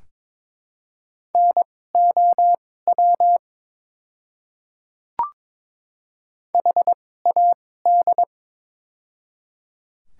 Morse code, heard as a keyed tone, about 23 wpm, spaced as for 16 wpm, 700 Hz. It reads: NOW HAD
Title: 2x Character Spacing - Top 500 Words - 22wpm